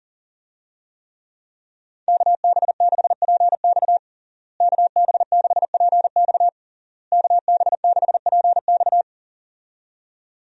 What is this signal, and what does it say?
Morse code, heard as a keyed tone, about 40 wpm, 700 Hz. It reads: KB6PX KB6PX KB6PX
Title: USA Call Signs - 5 Characters - Repeated 3x - 40wpm